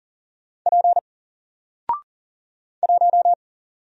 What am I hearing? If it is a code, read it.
Morse code, heard as a keyed tone, about 40 wpm, 700 Hz. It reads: P 1
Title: Single Letter-Number 40wpm